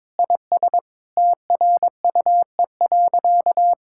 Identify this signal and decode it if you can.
Morse code, heard as a keyed tone, about 21 wpm, 700 Hz. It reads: IS TRUE.
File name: practice / 22wpm / Sentences with 4-Word Phrases - 22wpm